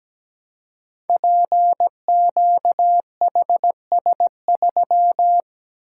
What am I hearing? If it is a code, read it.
Morse code, heard as a keyed tone, about 17 wpm, 700 Hz. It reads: PQHS3
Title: Code Groups - 5 Letter:Numbers - 17wpm